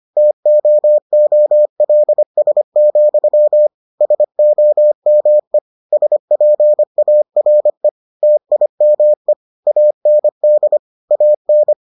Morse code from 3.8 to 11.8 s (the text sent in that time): SOME SPARE TIME AND AN